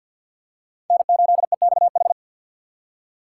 Morse code, heard as a keyed tone, about 46 wpm, 700 Hz. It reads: N8EXH
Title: USA Call Signs - 3x Character Spacing - 50wpm